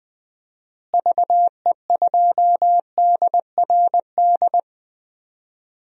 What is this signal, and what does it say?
Morse code, heard as a keyed tone, about 20 wpm, 700 Hz. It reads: VE2DRD
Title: DXCC-100 Call Signs - 6 Characters - Repeated 3x - 20wpm